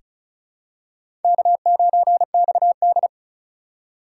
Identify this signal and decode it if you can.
Morse code, heard as a keyed tone, about 36 wpm, 700 Hz. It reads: K9XD
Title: K1USN SST QSO Practice - 35wpm